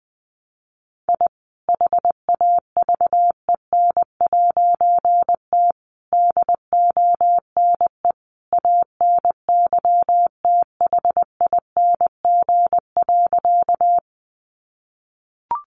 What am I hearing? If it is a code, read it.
Morse code, heard as a keyed tone, about 20 wpm, 700 Hz. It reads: I HAVEN'T DONE ANYTHING.
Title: A Sound Of Thunder 20wpm